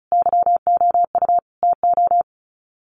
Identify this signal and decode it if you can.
Morse code, heard as a keyed tone, about 34 wpm, 700 Hz. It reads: YOU TO
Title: Sets of 5 Words Encore 35wpm